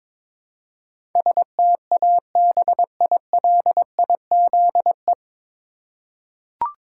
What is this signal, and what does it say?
Morse code, heard as a keyed tone, about 22 wpm, 700 Hz. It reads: STABILIZE
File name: 9 Letter Words - First 3 Letters Given - 22wpm